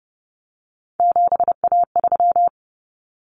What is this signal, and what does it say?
Morse code, heard as a keyed tone, about 30 wpm, 700 Hz. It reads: 7A3
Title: Common Cut Numbers - 3-Digit 30wpm